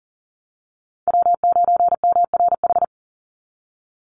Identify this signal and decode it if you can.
Morse code, heard as a keyed tone, about 40 wpm, 700 Hz. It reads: W9MRH
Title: NA SOTA Chaser Callsigns 40wpm